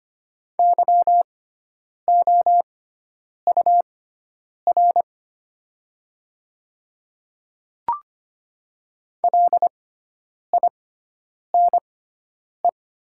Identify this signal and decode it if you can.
Morse code, heard as a keyed tone, about 25 wpm, spaced as for 9 wpm, 700 Hz. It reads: YOUR LINE
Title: 6x Character Spacing - Top 200 Words - 25wpm